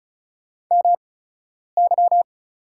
Morse code, heard as a keyed tone, about 34 wpm, spaced as for 10 wpm, 700 Hz. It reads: MY
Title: 8x Character Spacing - Top 300 Words - 35wpm